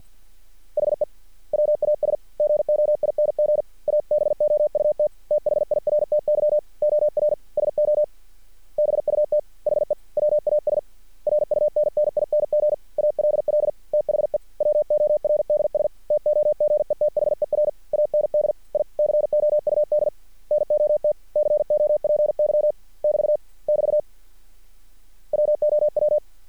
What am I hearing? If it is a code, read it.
Morse code, heard as a keyed tone, about 50 wpm, 600 Hz. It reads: HE WAS GOING ABOUT THIRTY OR SO BUT HE WAS RUNNING ALL THE WORDS TOGETHER AND I COUD NOT COPY <BT> <BT> WOW
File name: archived / CRSnet